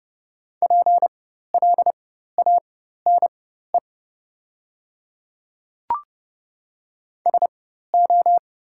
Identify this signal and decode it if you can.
Morse code, heard as a keyed tone, about 30 wpm, spaced as for 14 wpm, 690 Hz. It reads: PLANE SO
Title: 4x Character Spacing - Top 500 Words - 30wpm